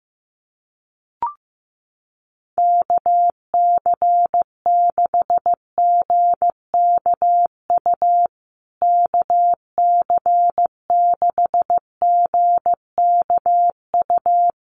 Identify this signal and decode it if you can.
Morse code, heard as a keyed tone, about 15 wpm, 700 Hz. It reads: KC6GKU KC6GKU
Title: USA Call Signs Repeated 3x 15wpm